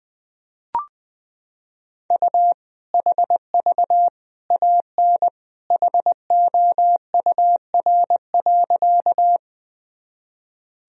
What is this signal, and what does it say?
Morse code, heard as a keyed tone, about 20 wpm, 700 Hz. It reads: U HV AN HOUR.
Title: Sentences from Top 400 Words and Abbreviations 20wpm